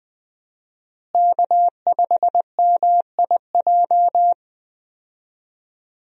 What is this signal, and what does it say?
Morse code, heard as a keyed tone, about 20 wpm, 700 Hz. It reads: K5MIJ
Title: USA Call Signs - 5 Characters - Repeated 3x - 20wpm